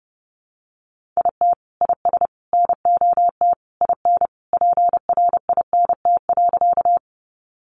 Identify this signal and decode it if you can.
Morse code, heard as a keyed tone, about 30 wpm, 700 Hz. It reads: IT IS NOT IN PRINT.
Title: Sentences from Top 1400 Words 30wpm